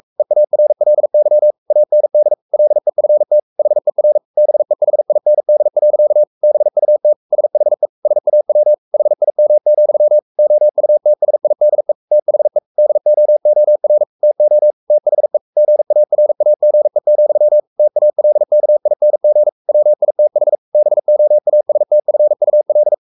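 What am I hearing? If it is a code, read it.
Morse code, heard as a keyed tone, about 44 wpm, 600 Hz. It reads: EARLY AND LEFT HER BEHIND. BUT SHE SAW HIM, OUTSIDE THE DOOR TO THE GARAGE, TALKING WITH BOASTFUL
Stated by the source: Morse code at 43 wpm sending Ken Follett's novel Never in English, from CW med Gnister.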